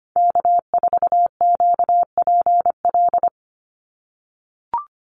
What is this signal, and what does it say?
Morse code, heard as a keyed tone, about 25 wpm, 700 Hz. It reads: K4QPL
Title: Call Signs - 25wpm